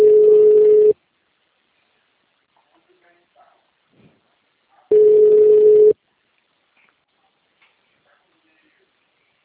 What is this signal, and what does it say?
Ringback tone (Europe, 425 Hz), with speech.
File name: out- -2015- - - .42315